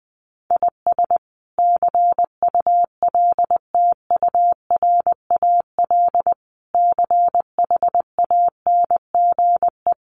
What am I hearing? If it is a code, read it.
Morse code, heard as a keyed tone, about 20 wpm, 700 Hz. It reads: IS CULTURAL CHANGE
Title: Sentences from Top 1500 Words 20wpm